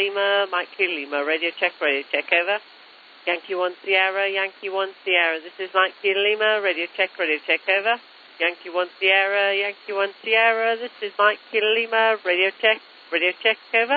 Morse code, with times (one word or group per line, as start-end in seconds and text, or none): none